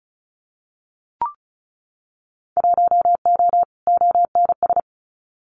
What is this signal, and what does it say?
Morse code, heard as a keyed tone, about 34 wpm, 700 Hz. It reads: 1O ONS
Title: Winter Field Day Exchanges 35wpm